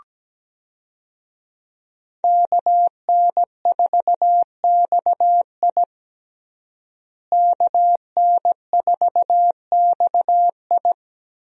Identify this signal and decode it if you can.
Morse code, heard as a keyed tone, about 17 wpm, 700 Hz. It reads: KN4XI KN4XI
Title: USA Call Signs - 5 Characters - Repeated 3x - 17wpm